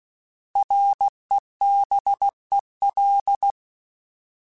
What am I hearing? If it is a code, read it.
Morse code, heard as a keyed tone, about 16 wpm, 800 Hz. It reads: REBEL
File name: Alphabetical / Alphabetical mp3s / Rs @15